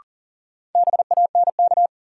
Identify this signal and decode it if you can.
Morse code, heard as a keyed tone, about 40 wpm, 700 Hz. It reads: BANK
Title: Sentences from Top 900 Words - New Word Drill - 40wpm